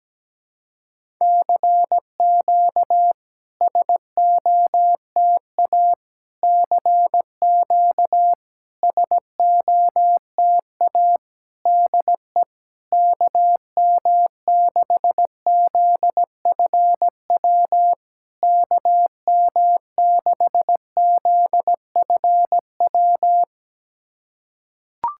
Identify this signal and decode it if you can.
Morse code, heard as a keyed tone, about 17 wpm, 700 Hz. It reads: CQ SOTA CQ SOTA DE KM6ZFW KM6ZFW
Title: SOTA QSO Practice 17wpm